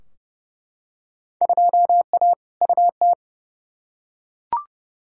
Morse code, heard as a keyed tone, about 31 wpm, 700 Hz. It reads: 2A UT
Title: Field Day Exchanges - 30wpm